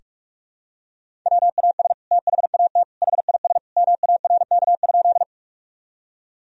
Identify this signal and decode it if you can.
Morse code, heard as a keyed tone, about 41 wpm, 700 Hz. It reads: WAS THAT HIS MARK?
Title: Sentences from Top 1000 Words - Review Part 1 - 45wpm